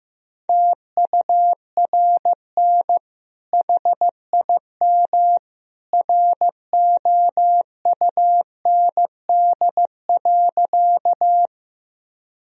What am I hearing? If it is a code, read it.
Morse code, heard as a keyed tone, about 15 wpm, 700 Hz. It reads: TURN HIM ROUND.
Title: Sentences from Top 1000 Words - Review Part 1 - 15wpm